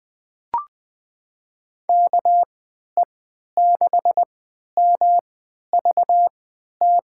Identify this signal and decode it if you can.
Morse code, heard as a keyed tone, about 20 wpm, spaced as for 11 wpm, 700 Hz. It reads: KE6MVT
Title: USA Call Signs - 3x Character Spacing - 20wpm